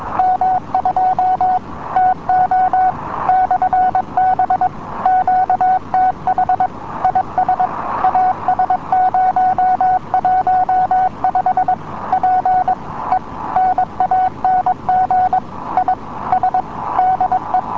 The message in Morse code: M2TO/B QTH IS AS015 PENANG IS DE

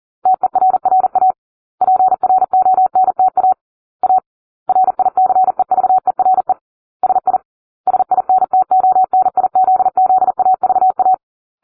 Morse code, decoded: TERRA PRONTA A RICEVERE SI SINTONIZZAVA

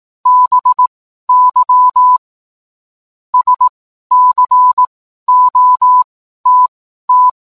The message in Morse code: BY SCOTT

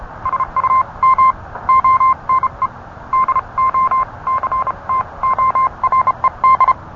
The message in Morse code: SUM ONE DOCTORED

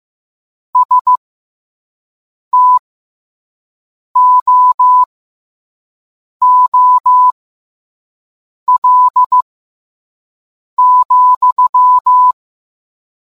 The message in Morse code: STOOL,

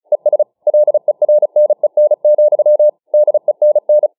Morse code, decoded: ES LERNEN, DENN

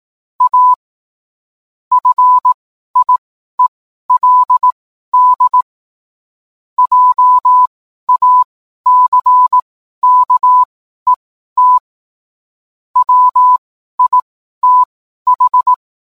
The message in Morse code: A FIELD JACKET WITH

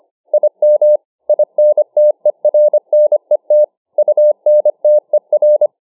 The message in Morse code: IM INTERNET UNTER

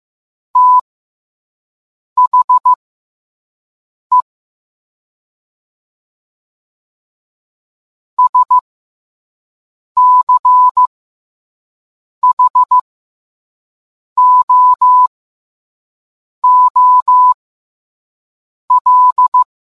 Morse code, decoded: THE SCHOOL

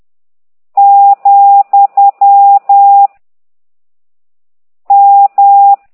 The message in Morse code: , M

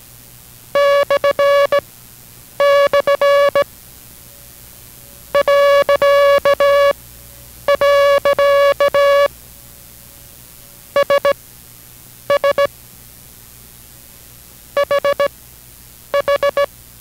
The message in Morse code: // .. SS HH

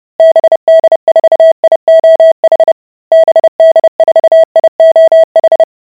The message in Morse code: BD4IOH BD4IOH

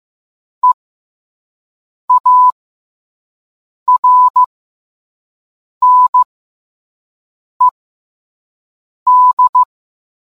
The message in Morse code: EARNED

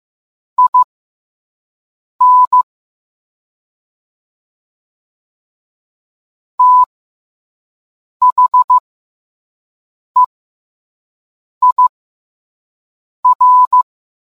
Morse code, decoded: IN THEIR

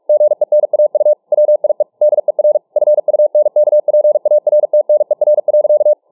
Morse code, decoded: GENAU WIE DER FUNKPARTNER.

